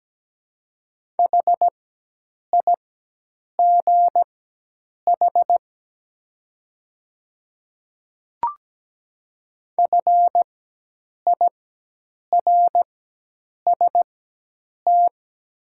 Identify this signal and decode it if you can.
Morse code, heard as a keyed tone, about 17 wpm, spaced as for 8 wpm, 700 Hz. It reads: HIGH FIRST
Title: 4x Character Spacing - Top 200 Words - 17wpm